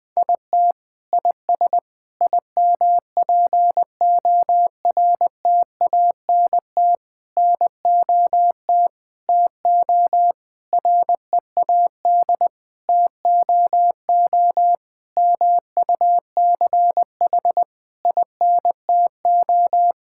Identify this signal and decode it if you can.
Morse code, heard as a keyed tone, about 21 wpm, 700 Hz. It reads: IT IS IMPORTANT NOT TO READ TOO MUCH INTO
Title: Sets of 9 Words - 20wpm